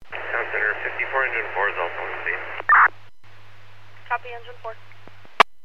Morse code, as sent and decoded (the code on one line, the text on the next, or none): none